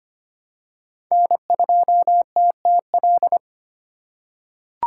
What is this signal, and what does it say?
Morse code, heard as a keyed tone, about 25 wpm, 700 Hz. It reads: N2TTL
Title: USA Call Signs - 5 Characters - 8x Character Spacing - 25wpm